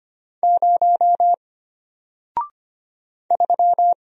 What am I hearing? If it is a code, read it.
Morse code, heard as a keyed tone, about 25 wpm, 700 Hz. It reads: 0 3
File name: Single-Letter-Number-25wpm